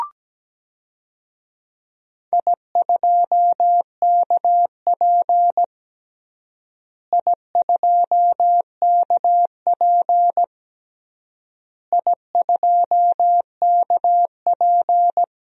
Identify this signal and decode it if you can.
Morse code, heard as a keyed tone, about 17 wpm, 700 Hz. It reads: I2KP I2KP I2KP
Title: DXCC-100 Call Signs - 4 Characters - Repeated 3x - 17wpm